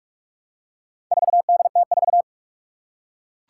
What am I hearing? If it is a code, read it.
Morse code, heard as a keyed tone, about 44 wpm, 700 Hz. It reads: 4DT4
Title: Four Challenging Letter-Number Combinations 45wpm